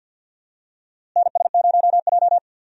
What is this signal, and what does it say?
Morse code, heard as a keyed tone, about 48 wpm, 700 Hz. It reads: NS0J
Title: USA Call Signs - 4 Characters - Repeated 2x - 50wpm